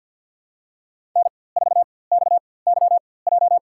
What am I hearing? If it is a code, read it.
Morse code, heard as a keyed tone, about 52 wpm, spaced as for 23 wpm, 700 Hz. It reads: N4XYJ
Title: USA Call Signs - 4x Character Spacing - 50wpm